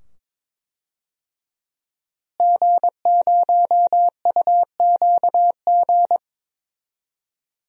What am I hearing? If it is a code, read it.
Morse code, heard as a keyed tone, about 21 wpm, 700 Hz. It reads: G0UQG
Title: DXCC-100 Call Signs - 5 Characters - Repeated 2x - 22wpm